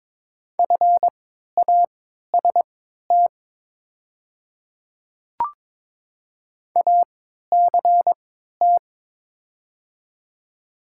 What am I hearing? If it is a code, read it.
Morse code, heard as a keyed tone, about 22 wpm, spaced as for 12 wpm, 700 Hz. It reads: FAST ACT
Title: 3x Character Spacing - Top 500 Words - 22wpm